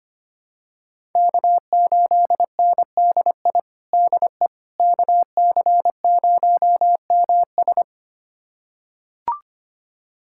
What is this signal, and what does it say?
Morse code, heard as a keyed tone, about 25 wpm, 700 Hz. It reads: K8NDI DE KC0MS
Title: USA Callsign DE Callsign - 5 Characters - 25wpm